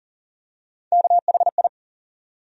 Morse code, heard as a keyed tone, about 40 wpm, 700 Hz. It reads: KHI
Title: USA Call Sign Postfixes - 3 Letters 40wpm